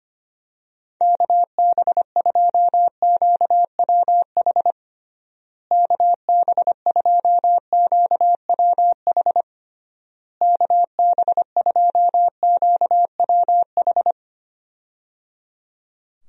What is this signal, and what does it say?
Morse code, heard as a keyed tone, about 25 wpm, 700 Hz. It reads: KB2QWH KB2QWH KB2QWH
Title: USA Call Signs - 6 Characters - Repeated 3x - 25wpm